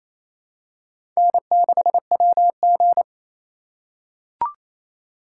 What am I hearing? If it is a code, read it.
Morse code, heard as a keyed tone, about 28 wpm, 700 Hz. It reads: N6WG
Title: USA Call Signs - 4 Characters - 4x Character Spacing - 28wpm